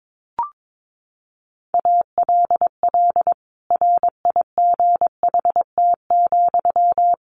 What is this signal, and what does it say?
Morse code, heard as a keyed tone, about 22 wpm, 700 Hz. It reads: ALL RIGHT,
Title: A Sound Of Thunder 22wpm